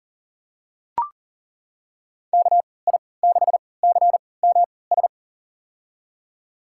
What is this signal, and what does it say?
Morse code, heard as a keyed tone, about 40 wpm, spaced as for 23 wpm, 700 Hz. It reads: KI6CMS
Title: USA Call Signs - 3x Character Spacing - 40wpm